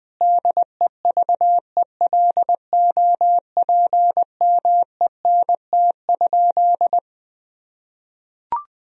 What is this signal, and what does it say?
Morse code, heard as a keyed tone, about 20 wpm, 700 Hz. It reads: DEVELOPMENT?
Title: Sentences from Top 1000 Words - Review Part 4 - 20wpm